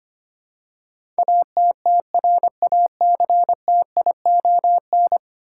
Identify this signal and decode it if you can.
Morse code, heard as a keyed tone, about 25 wpm, 700 Hz. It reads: ATTRACTION
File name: Ten Letter Words - 25wpm